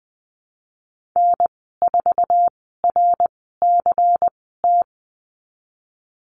Morse code, heard as a keyed tone, about 20 wpm, spaced as for 15 wpm, 700 Hz. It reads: N4RCT
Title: 128 Lesson 21g - Number 6 - Call Signs 20wpm